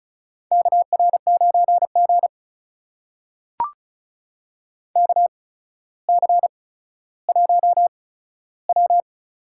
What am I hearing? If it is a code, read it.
Morse code, heard as a keyed tone, about 34 wpm, spaced as for 15 wpm, 700 Hz. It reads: KR9G KC1W
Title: USA Call Signs - 4 Characters - 8x Character Spacing - 35wpm